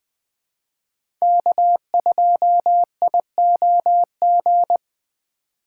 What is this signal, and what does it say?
Morse code, heard as a keyed tone, about 20 wpm, 700 Hz. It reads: K2IOG
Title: Speed Racing - USA 1x3 Call Signs 17wpm